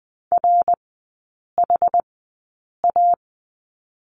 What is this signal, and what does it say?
Morse code, heard as a keyed tone, about 20 wpm, spaced as for 8 wpm, 700 Hz. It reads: RHA
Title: 074 Lesson 13e - Letter L - Three Characters 20wpm